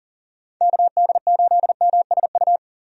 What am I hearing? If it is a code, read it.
Morse code, heard as a keyed tone, about 40 wpm, 700 Hz. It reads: KD8MSU
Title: USA Call Signs - 6 Characters - Repeated 3x - 40wpm